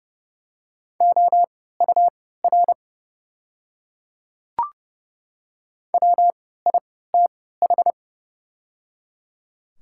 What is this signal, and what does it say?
Morse code, heard as a keyed tone, about 30 wpm, spaced as for 17 wpm, 690 Hz. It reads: OUR WITH
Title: 3x Character Spacing - Top 300 Words - 30wpm